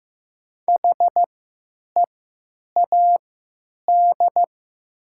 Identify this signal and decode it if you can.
Morse code, heard as a keyed tone, about 15 wpm, spaced as for 9 wpm, 700 Hz. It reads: HEAD
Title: 3x Character Spacing - Top 500 Words - 15wpm